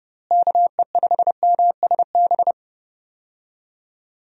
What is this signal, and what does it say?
Morse code, heard as a keyed tone, about 30 wpm, 700 Hz. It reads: KE5MSB
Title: USA Call Signs - 6 Characters - Repeated 3x - 30wpm